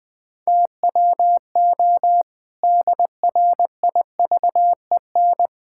Morse code, heard as a keyed tone, about 20 wpm, 700 Hz. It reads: TWO DRIVEN
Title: Pangram Challenge v2 - 20wpm